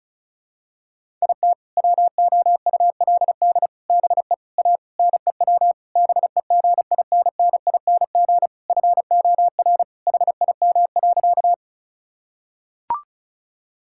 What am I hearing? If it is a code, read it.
Morse code, heard as a keyed tone, about 34 wpm, 700 Hz. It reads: IT WOULD BE A NEW BEGINNING FOR HIM.
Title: Sentences from Top 1000 Words - Review Part 5 - 35wpm